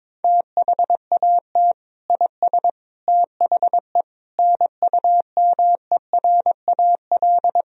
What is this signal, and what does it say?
Morse code, heard as a keyed tone, about 22 wpm, 700 Hz. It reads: THAT IS THE NUMERAL
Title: Sentences from Top 1000 Words - Review Part 2 - 22wpm